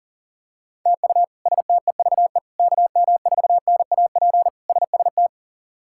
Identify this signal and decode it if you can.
Morse code, heard as a keyed tone, about 37 wpm, 700 Hz. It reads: TU STEVE KM4NAP SST
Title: K1USN SST QSO Practice - 40wpm